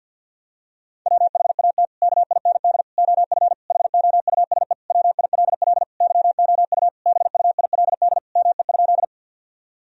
Morse code, heard as a keyed tone, about 49 wpm, 700 Hz. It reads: WHAT KIND OF HOUSE WILL YOU BUILD ME?
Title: Sentences from Top 1000 Words - Review Part 1 - 50wpm